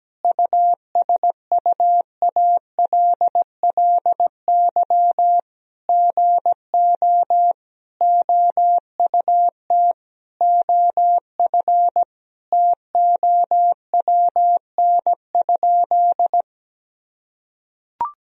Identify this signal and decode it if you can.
Morse code, heard as a keyed tone, about 17 wpm, 700 Hz. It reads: USUALLY GO OUT OF TOWN?
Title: Sentences from Top 1000 Words - Review Part 3 - 17wpm